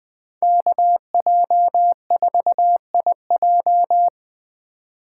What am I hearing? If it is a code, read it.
Morse code, heard as a keyed tone, about 20 wpm, 700 Hz. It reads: KJ4IJ
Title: USA Call Signs - 5 Characters - 2x Character Spacing - 20wpm